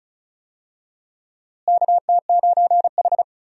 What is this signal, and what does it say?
Morse code, heard as a keyed tone, about 34 wpm, 700 Hz. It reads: KT9H
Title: USA Call Signs - 4 Characters - Repeated 3x - 35wpm